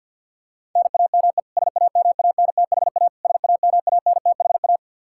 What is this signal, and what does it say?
Morse code, heard as a keyed tone, about 52 wpm, 700 Hz. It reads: NAME SAMANTHA SAMANTHA
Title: Practice QSO - Exchange 1 - Beginning - 50wpm